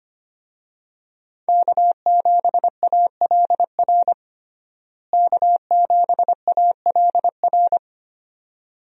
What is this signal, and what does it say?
Morse code, heard as a keyed tone, about 25 wpm, 700 Hz. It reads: K7ALR K7ALR
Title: USA Call Signs - 5 Characters - Repeated 2x - 25wpm